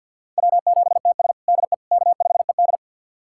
Morse code, heard as a keyed tone, about 46 wpm, 700 Hz. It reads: W7TS DE K5ED
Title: USA Callsign DE Callsign - 4 Characters - 50wpm